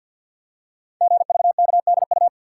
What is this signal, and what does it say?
Morse code, heard as a keyed tone, about 45 wpm, 700 Hz. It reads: GVKDU
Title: Code Groups - 5 Letter:Numbers - 50wpm